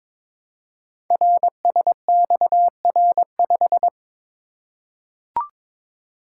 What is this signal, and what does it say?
Morse code, heard as a keyed tone, about 23 wpm, 700 Hz. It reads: RSXR5
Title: Code Groups - 5 Letter:Numbers - 22wpm